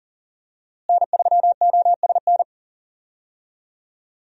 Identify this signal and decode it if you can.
Morse code, heard as a keyed tone, about 38 wpm, 700 Hz. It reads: N3OSN
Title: USA Call Signs - 5 Characters - Repeated 2x - 40wpm